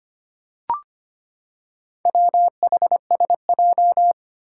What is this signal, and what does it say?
Morse code, heard as a keyed tone, about 25 wpm, 700 Hz. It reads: WHSJ
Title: Four Challenging Letter-Number Combinations 25wpm